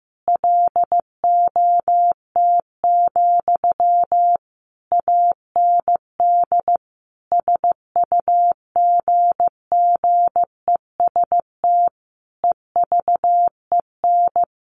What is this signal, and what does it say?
Morse code, heard as a keyed tone, about 15 wpm, 700 Hz. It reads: LOT, AND SUGGEST EVEN
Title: Sentences from Top 1300 Words 15wpm